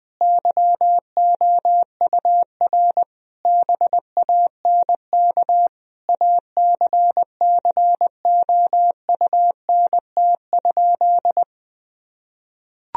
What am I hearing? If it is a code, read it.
Morse code, heard as a keyed tone, about 21 wpm, 700 Hz. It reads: YOUR BANK ACCOUNT?
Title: Sentences from Top 1000 Words - Review Part 4 - 20wpm